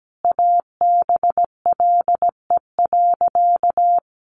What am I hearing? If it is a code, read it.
Morse code, heard as a keyed tone, about 17 wpm, 700 Hz. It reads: ABLE.
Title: Sentences from Top 1300 Words 17wpm